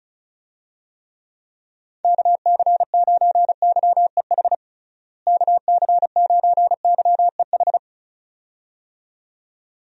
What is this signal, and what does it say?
Morse code, heard as a keyed tone, about 36 wpm, 700 Hz. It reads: KC9YEH KC9YEH
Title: USA Call Signs - 6 Characters - Repeated 2x - 35wpm